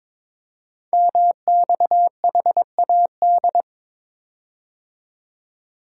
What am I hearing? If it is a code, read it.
Morse code, heard as a keyed tone, about 22 wpm, 700 Hz. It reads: MXHAD
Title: Code Groups - 5 Letters - 22wpm